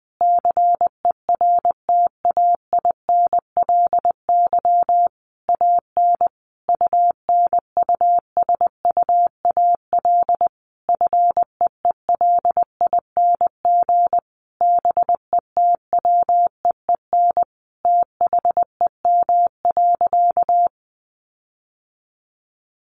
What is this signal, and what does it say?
Morse code, heard as a keyed tone, about 20 wpm, 700 Hz. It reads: CERTAINLY AN UNUSUAL FEELING BETWEEN THEM.
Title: Sentences from Top 1400 Words 20wpm